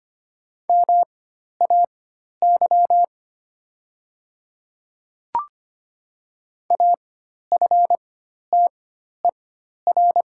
Morse code, heard as a keyed tone, about 25 wpm, spaced as for 12 wpm, 690 Hz. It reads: MAY AFTER
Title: 4x Character Spacing - Top 500 Words - 25wpm